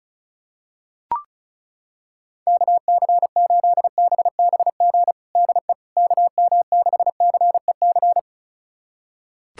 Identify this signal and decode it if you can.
Morse code, heard as a keyed tone, about 36 wpm, 690 Hz. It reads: KC8BBG DE KM6CEC
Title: USA Callsign DE Callsign 35wpm